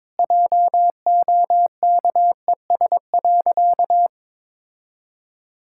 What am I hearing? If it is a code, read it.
Morse code, heard as a keyed tone, about 22 wpm, 700 Hz. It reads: JOKES.
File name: practice / 22wpm / Sentences with 2-Word Phrases - 22wpm